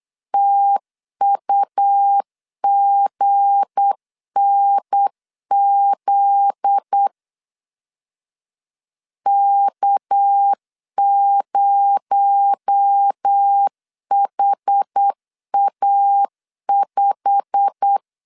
Morse code, heard as a keyed tone, about 8 wpm, 800 Hz. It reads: TUGNZ K0HA5
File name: CW / DET / 1 9 1500C S B 8wpm 5624